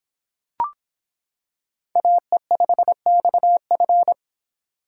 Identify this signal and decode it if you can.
Morse code, heard as a keyed tone, about 26 wpm, 700 Hz. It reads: AE5XF